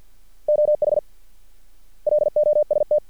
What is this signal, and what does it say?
Morse code, heard as a keyed tone, about 50 wpm, 600 Hz. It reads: OH LOST
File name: archived / CRSnet